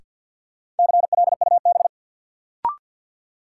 Morse code, heard as a keyed tone, about 53 wpm, 700 Hz. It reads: CLUB
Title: Sentences from Top 1200 Words - New Word Drill Alt - 50wpm